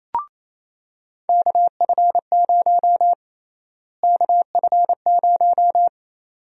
Morse code, heard as a keyed tone, about 28 wpm, 700 Hz. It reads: KF0 KF0
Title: USA Call Sign Prefixes - Repeated 2x - 28wpm